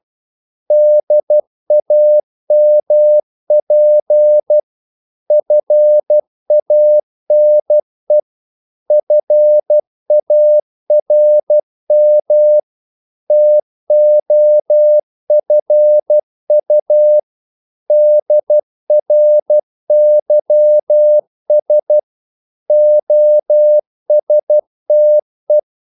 Morse code, heard as a keyed tone, about 12 wpm, 600 Hz. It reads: DAMP FANE FARM TOFU DRYS OSTE
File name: Korte ord DK 12wpm